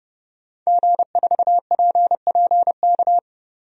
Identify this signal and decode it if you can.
Morse code, heard as a keyed tone, about 30 wpm, 700 Hz. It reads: G4PPK
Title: DXCC-100 Call Signs - 5 Characters - Repeated 2x - 30wpm